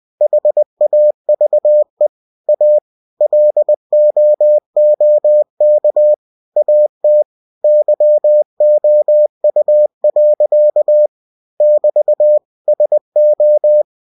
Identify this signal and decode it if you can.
Morse code, heard as a keyed tone, about 20 wpm, 600 Hz. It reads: HAVE A LOOK AT YOU. <BT> SO
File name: cw-books / winnie-the-pooh / winnie-20wpm-page-0026